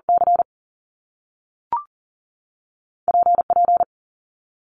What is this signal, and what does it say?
Morse code, heard as a keyed tone, about 42 wpm, 700 Hz. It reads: C P P
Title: Top Two Letter Combinations in Words 40wpm